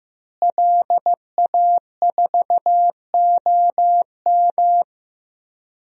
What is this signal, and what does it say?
Morse code, heard as a keyed tone, about 15 wpm, 700 Hz. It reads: LA4OM
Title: DXCC-100 Call Signs - 5 Characters - Repeated 2x - 15wpm